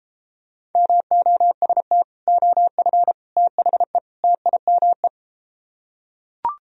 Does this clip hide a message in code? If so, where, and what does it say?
Yes: Morse code from 0.8 to 5.1 s, heard as a keyed tone, about 33 wpm, 700 Hz. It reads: MOST OF THE TIME